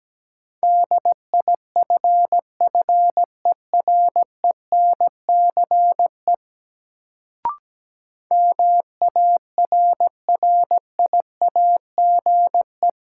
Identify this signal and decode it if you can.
Morse code, heard as a keyed tone, about 17 wpm, 700 Hz. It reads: DIFFERENCE MARRIAGE
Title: Sentences from Top 800 Words - New Word Drill - 17wpm